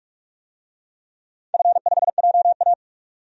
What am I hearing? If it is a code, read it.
Morse code, heard as a keyed tone, about 46 wpm, 700 Hz. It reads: F5JA
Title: DXCC-100 Call Signs - 4 Characters - 45wpm